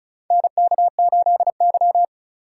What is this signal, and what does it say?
Morse code, heard as a keyed tone, about 36 wpm, 700 Hz. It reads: NK8Y
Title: USA Call Signs - 4 Characters - 3x Character Spacing - 35wpm